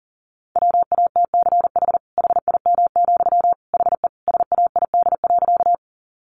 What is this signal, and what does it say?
Morse code, heard as a keyed tone, about 37 wpm, 700 Hz. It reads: WATCH HIM, HE SAID.
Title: Sentences from Top 300 Words 40wpm